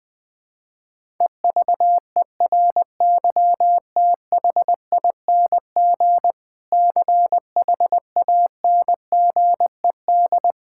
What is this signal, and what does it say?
Morse code, heard as a keyed tone, about 20 wpm, 700 Hz. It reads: EVERYTHING CHANGED
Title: Sentences with 6-Word Phrases - 20wpm